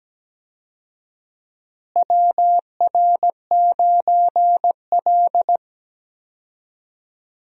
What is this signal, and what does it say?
Morse code, heard as a keyed tone, about 17 wpm, 700 Hz. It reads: WR9L
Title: USA Call Signs - 4 Characters - Repeated 3x - 17wpm